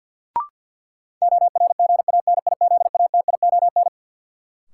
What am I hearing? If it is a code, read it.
Morse code, heard as a keyed tone, about 52 wpm, 700 Hz. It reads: ORGANIZATION
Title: Sentences from Top 1100 Words - New Word Drill Alt - 50wpm